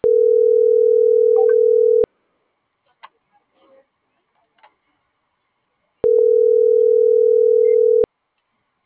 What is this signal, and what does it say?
Ringback tone (North America), 440 + 480 Hz.